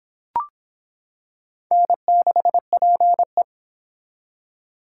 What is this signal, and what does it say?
Morse code, heard as a keyed tone, about 25 wpm, 700 Hz. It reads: N6PE